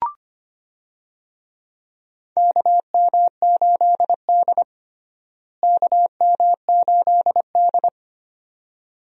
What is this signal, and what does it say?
Morse code, heard as a keyed tone, about 25 wpm, 700 Hz. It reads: KM8D KM8D
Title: USA Call Signs - 4 Characters - Repeated 2x - 25wpm